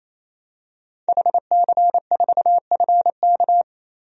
Morse code, heard as a keyed tone, about 28 wpm, 700 Hz. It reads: HC4FK
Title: DXCC-100 Call Signs - 5 Characters - 28wpm